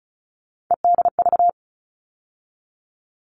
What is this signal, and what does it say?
Morse code, heard as a keyed tone, about 34 wpm, 700 Hz. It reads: EDV
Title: All Cut Numbers - 3-Digit 35wpm